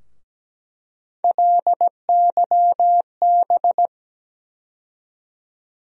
Morse code, heard as a keyed tone, about 17 wpm, 700 Hz. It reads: LYB
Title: USA Call Sign Postfixes - 3 Letters 17wpm